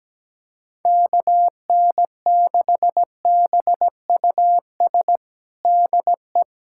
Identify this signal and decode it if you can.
Morse code, heard as a keyed tone, about 17 wpm, 700 Hz. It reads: KN6BUS DE
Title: USA Callsign DE Callsign - 6 Characters - 17wpm